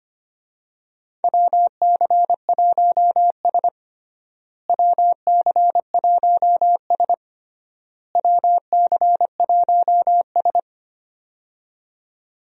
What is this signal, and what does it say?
Morse code, heard as a keyed tone, about 25 wpm, 700 Hz. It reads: WC1S WC1S WC1S
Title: USA Call Signs - 4 Characters - Repeated 3x - 25wpm